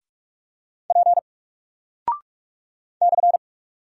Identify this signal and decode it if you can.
Morse code, heard as a keyed tone, about 44 wpm, 700 Hz. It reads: P /
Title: Single Character (Letter:Number:Punctuation:Prosign) Alt 45wpm